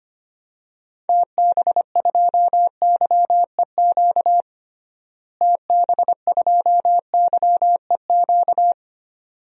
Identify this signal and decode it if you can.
Morse code, heard as a keyed tone, about 25 wpm, 700 Hz. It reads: TB2YEQ TB2YEQ
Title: DXCC-100 Call Signs - 6 Characters - Repeated 2x - 25wpm